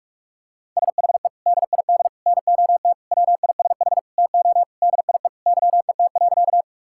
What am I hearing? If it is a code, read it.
Morse code, heard as a keyed tone, about 45 wpm, 700 Hz. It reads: SHE DID NOT WISH TO DIE YET.
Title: Sentences from Top 1600 Words 45wpm